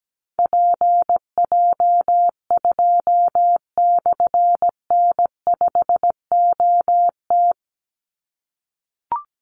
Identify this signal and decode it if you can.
Morse code, heard as a keyed tone, about 17 wpm, 700 Hz. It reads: PJ2/N5OT
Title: Call Signs - 17wpm